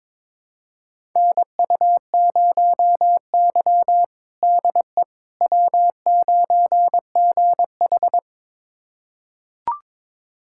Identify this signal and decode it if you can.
Morse code, heard as a keyed tone, about 22 wpm, 700 Hz. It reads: NU0Y DE W9GH
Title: USA Callsign DE Callsign - 4 Characters - 22wpm